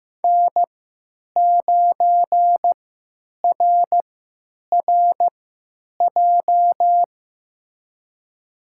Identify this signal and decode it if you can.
Morse code, heard as a keyed tone, about 15 wpm, spaced as for 9 wpm, 700 Hz. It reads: N9RRJ
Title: USA Call Signs - 5 Characters - 3x Character Spacing - 15wpm